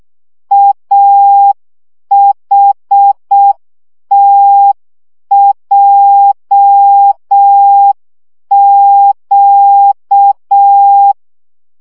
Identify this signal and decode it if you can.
Morse code, heard as a keyed tone, about 6 wpm, 800 Hz. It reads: AHTJQ
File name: morze vizsga 2011-3 5